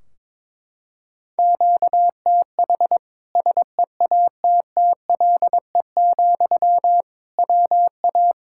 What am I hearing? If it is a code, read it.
Morse code, heard as a keyed tone, about 22 wpm, 700 Hz. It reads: QTH SEATTLE, WA
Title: Practice QSO - Exchange 1 - Beginning - 22wpm